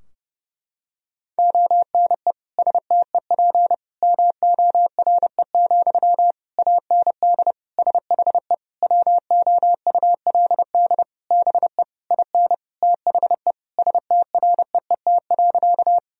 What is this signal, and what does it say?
Morse code, heard as a keyed tone, about 30 wpm, 700 Hz. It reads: ONE STEP MORE, AND SHE WOULD BE IN THE STREET.
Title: Sentences from Top 1000 Words - Review Part 3 - 30wpm